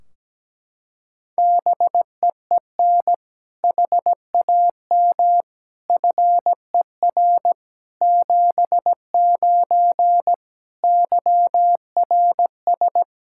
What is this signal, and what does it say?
Morse code, heard as a keyed tone, about 17 wpm, 700 Hz. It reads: BEEN HAM FER 79 YRS
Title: Practice QSO - Exchange 3 - More Info - 17wpm